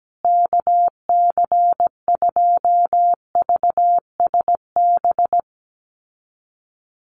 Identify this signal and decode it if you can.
Morse code, heard as a keyed tone, about 17 wpm, 700 Hz. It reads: KC2VSB
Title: USA Call Signs Repeated 3x 17wpm